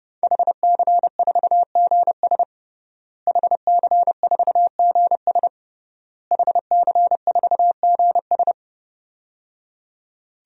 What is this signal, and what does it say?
Morse code, heard as a keyed tone, about 30 wpm, 700 Hz. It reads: HC4GS HC4GS HC4GS
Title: DXCC-100 Call Signs - 5 Characters - Repeated 3x - 30wpm